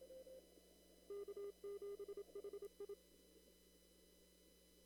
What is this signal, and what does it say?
Morse code, heard as a keyed tone, about 26 wpm, 400 Hz. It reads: K7HI
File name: CW / kt5j wpx cw-op-k5pi